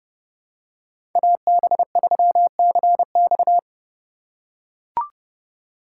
Morse code, heard as a keyed tone, about 30 wpm, 700 Hz. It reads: AB3CX
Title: USA Call Signs - 5 Characters - 3x Character Spacing - 30wpm